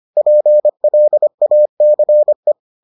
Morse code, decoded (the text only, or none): PLACE